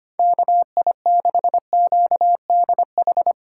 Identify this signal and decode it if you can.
Morse code, heard as a keyed tone, about 25 wpm, 700 Hz. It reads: KI6QDH
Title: K1USN SST QSO Practice - 25wpm